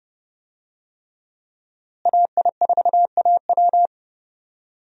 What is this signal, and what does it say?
Morse code, heard as a keyed tone, about 32 wpm, 700 Hz. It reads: AI4AW
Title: USA Call Signs - 5 Characters - Repeated 3x - 30wpm